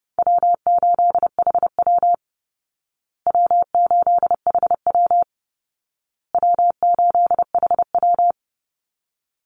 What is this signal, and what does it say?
Morse code, heard as a keyed tone, about 30 wpm, 700 Hz. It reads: W8HW W8HW W8HW
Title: Easy Call Signs Repeated 3x - 30wpm